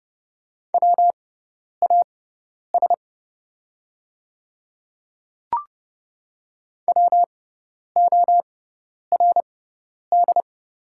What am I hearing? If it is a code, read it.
Morse code, heard as a keyed tone, about 30 wpm, spaced as for 10 wpm, 700 Hz. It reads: WAS WORD
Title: 6x Character Spacing - Top 100 Words - 30wpm